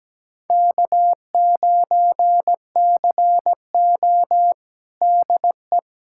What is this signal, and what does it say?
Morse code, heard as a keyed tone, about 17 wpm, 700 Hz. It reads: K9CO DE
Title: USA Callsign DE Callsign - 4 Characters - 17wpm